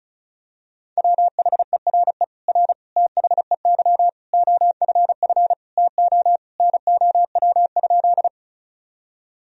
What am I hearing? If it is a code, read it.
Morse code, heard as a keyed tone, about 34 wpm, 700 Hz. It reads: WHERE R THEY OFF TO NOW?
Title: Sentences from Top 200 Words and Abbreviations 35wpm